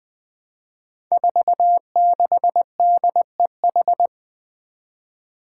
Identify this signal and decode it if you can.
Morse code, heard as a keyed tone, about 20 wpm, 700 Hz. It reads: 46DEH
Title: Five Challenging Letter-Number Combinations 20wpm